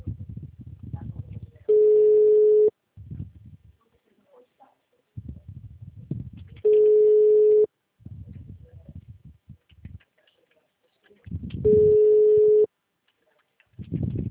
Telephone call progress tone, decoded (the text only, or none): ringback tone